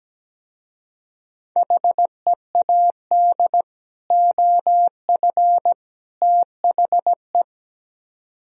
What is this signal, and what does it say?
Morse code, heard as a keyed tone, about 17 wpm, 700 Hz. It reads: HEAD OF THE
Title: Speed-Racing - Sets of 3 Words 17wpm